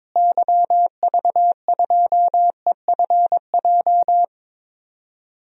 Morse code, heard as a keyed tone, about 21 wpm, 700 Hz. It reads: YV2EFJ
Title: DXCC-100 Call Signs - 6 Characters - Repeated 2x - 22wpm